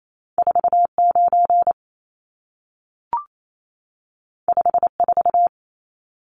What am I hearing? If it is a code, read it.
Morse code, heard as a keyed tone, about 28 wpm, 700 Hz. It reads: 49 54
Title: Numbers - 2-Digit 28wpm